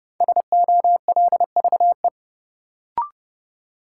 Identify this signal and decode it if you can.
Morse code, heard as a keyed tone, about 30 wpm, 700 Hz. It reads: SOLVE
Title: Sentences from Top 1000 Words - New Word Drill - 30wpm